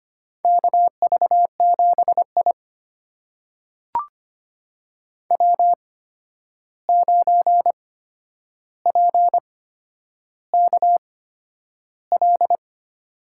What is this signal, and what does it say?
Morse code, heard as a keyed tone, about 25 wpm, spaced as for 10 wpm, 700 Hz. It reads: KV7I W9PKL
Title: USA Call Signs - 8x Character Spacing - 25wpm